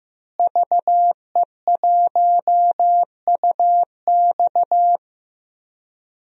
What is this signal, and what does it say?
Morse code, heard as a keyed tone, about 15 wpm, 700 Hz. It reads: VE1UX
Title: DXCC-100 Call Signs - 5 Characters - Repeated 3x - 15wpm